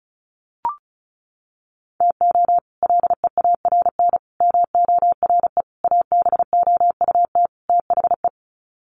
Morse code, heard as a keyed tone, about 34 wpm, 700 Hz. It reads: TO LEARN MORE ABOUT THE
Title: Sets of 5 Words Encore2 35wpm